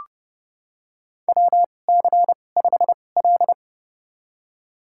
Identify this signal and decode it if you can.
Morse code, heard as a keyed tone, about 31 wpm, spaced as for 22 wpm, 700 Hz. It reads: WC5L
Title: USA Call Signs - 4 Characters - 2x Character Spacing - 30wpm